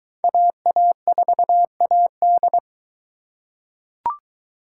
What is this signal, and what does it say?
Morse code, heard as a keyed tone, about 22 wpm, 700 Hz. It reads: AA4AD